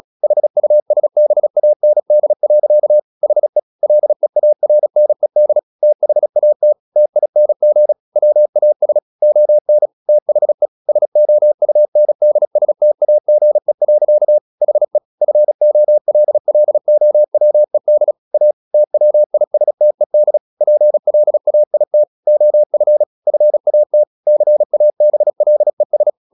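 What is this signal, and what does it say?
Morse code, heard as a keyed tone, about 36 wpm, 600 Hz. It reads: HUSBAND. HE LEARNED THAT TING WAS ON THE SOUNDSTAGE. HE FOLLOWED A TWISTED PLAIT OF FAT CABLES
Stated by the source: Morse practice at 36 wpm, text of Ken Follett's novel Never, from CW med Gnister.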